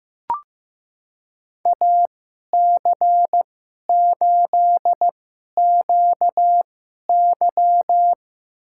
Morse code, heard as a keyed tone, about 15 wpm, spaced as for 11 wpm, 700 Hz. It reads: AC8QY
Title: USA Call Signs - 5 Characters - 2x Character Spacing - 15wpm